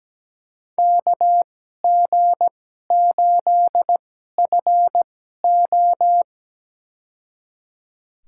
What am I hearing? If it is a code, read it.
Morse code, heard as a keyed tone, about 17 wpm, spaced as for 12 wpm, 700 Hz. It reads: KG8FO
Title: USA Call Signs - 5 Characters - 2x Character Spacing - 17wpm